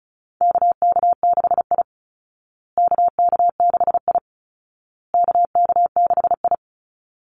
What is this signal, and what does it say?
Morse code, heard as a keyed tone, about 34 wpm, 700 Hz. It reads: KK6I KK6I KK6I
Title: Call Signs Repeated 2x - 35wpm